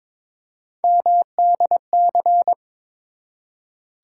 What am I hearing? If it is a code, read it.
Morse code, heard as a keyed tone, about 23 wpm, 700 Hz. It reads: MDC
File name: ARRL Sections v2 - 22wpm